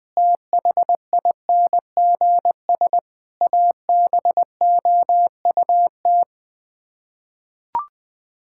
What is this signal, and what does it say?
Morse code, heard as a keyed tone, about 20 wpm, 700 Hz. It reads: THINGS ABOUT
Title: Sets of 6 Words - 20wpm